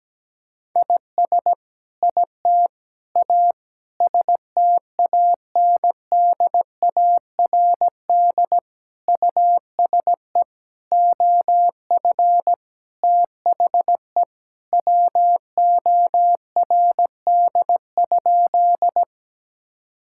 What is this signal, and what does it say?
Morse code, heard as a keyed tone, about 17 wpm, 700 Hz. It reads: IS IT A STANDARD USE OF THE WORD?
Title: Sentences from Top 1000 Words - Review Part 6 - 17wpm